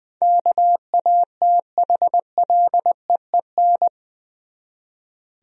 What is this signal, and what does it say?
Morse code, heard as a keyed tone, about 20 wpm, 700 Hz. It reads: KATHLEEN
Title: USA Common Names - 20wpm